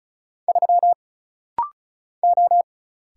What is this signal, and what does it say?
Morse code, heard as a keyed tone, about 36 wpm, 700 Hz. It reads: 3 O
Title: Single Letter-Number Alt - Rapid-Fire 35wpm